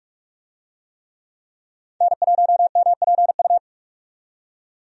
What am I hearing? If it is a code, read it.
Morse code, heard as a keyed tone, about 44 wpm, 700 Hz. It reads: N1MPU
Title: USA Call Signs - 5 Characters - 45wpm